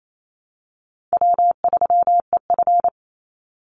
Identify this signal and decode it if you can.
Morse code, heard as a keyed tone, about 28 wpm, 700 Hz. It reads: W3EF
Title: Easy Call Signs Repeated 2x - 28wpm